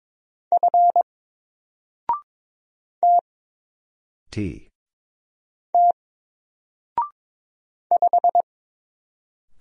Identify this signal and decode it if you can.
Morse code, heard as a keyed tone, about 22 wpm, 700 Hz. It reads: F T T 5
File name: Single-Letter-Number-22wpm